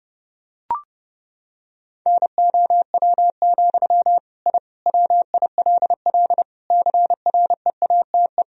Morse code, heard as a keyed tone, about 30 wpm, 700 Hz. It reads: NOW, I WILL CREATE
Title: Sentences from Top 1000 Words - Review Part 3 - 30wpm